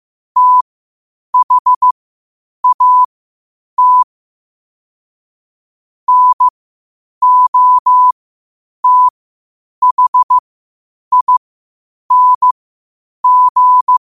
Morse code, heard as a keyed tone, about 15 wpm, spaced as for 8 wpm, 1000 Hz. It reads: THAT NOTHING
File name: qotdfiles / 8wpm